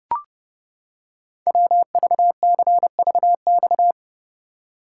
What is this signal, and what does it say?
Morse code, heard as a keyed tone, about 30 wpm, 700 Hz. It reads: WVCVX
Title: Code Groups - 5 Letter:Numbers - 30wpm